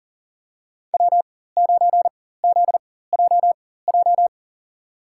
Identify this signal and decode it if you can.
Morse code, heard as a keyed tone, about 38 wpm, spaced as for 18 wpm, 700 Hz. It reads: W9ZJJ
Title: USA Call Signs - 5 Characters - 4x Character Spacing - 40wpm